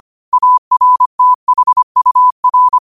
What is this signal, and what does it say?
Morse code, heard as a keyed tone, about 25 wpm, 1000 Hz. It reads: ARTHUR